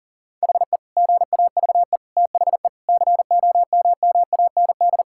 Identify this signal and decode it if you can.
Morse code, heard as a keyed tone, about 40 wpm, 700 Hz. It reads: HE GAVE THE COMMAND
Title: Sentences from Top 1600 Words 40wpm